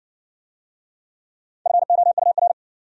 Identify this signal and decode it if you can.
Morse code, heard as a keyed tone, about 60 wpm, 700 Hz. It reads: FOUR